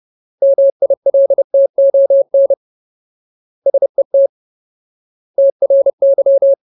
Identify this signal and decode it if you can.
Morse code, heard as a keyed tone, about 30 wpm, 550 Hz. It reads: MILTON SET TRY
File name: Ultimate Word List / 30WPM Ultimate